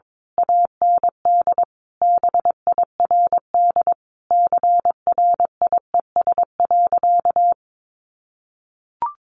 Morse code, heard as a keyed tone, about 22 wpm, 700 Hz. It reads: AND BIRD CRIES.
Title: A Sound Of Thunder 22wpm